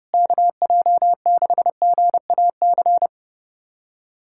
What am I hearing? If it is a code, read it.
Morse code, heard as a keyed tone, about 32 wpm, 700 Hz. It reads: KJ6GAC
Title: USA Callsign DE Callsign - 6 Characters - 30wpm